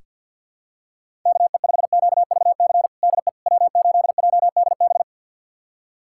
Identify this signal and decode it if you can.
Morse code, heard as a keyed tone, about 46 wpm, 700 Hz. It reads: KE5QVC DE W8JDB
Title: USA Callsign DE Callsign 50wpm